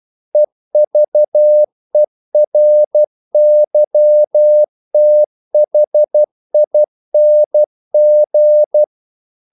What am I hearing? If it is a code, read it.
Morse code, heard as a keyed tone, about 12 wpm, 600 Hz. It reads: EVERYTHING
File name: CW / Practice / EnglishQuotations0112